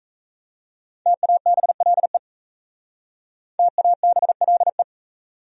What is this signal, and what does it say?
Morse code, heard as a keyed tone, about 37 wpm, 700 Hz. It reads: TABLE TABLE